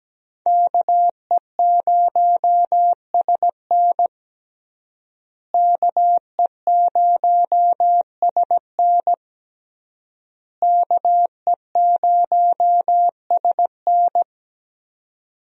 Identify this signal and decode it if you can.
Morse code, heard as a keyed tone, about 17 wpm, 700 Hz. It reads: KE0SN KE0SN KE0SN
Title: USA Call Signs - 5 Characters - Repeated 3x - 17wpm